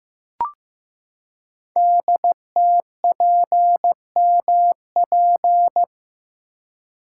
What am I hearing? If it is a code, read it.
Morse code, heard as a keyed tone, about 15 wpm, 700 Hz. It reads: DTPMP
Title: Code Groups - 5 Letters - 15wpm